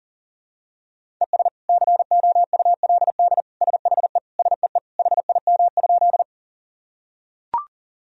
Morse code, heard as a keyed tone, about 40 wpm, 700 Hz. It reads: ES COULD SHE SEE HIM?
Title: Sentences from Top 100 Words and Abbreviations 40wpm